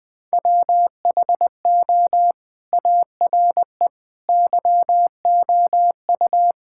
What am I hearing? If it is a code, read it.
Morse code, heard as a keyed tone, about 20 wpm, 700 Hz. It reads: WHO ARE YOU
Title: Sentences with 2-Word Phrases - 20wpm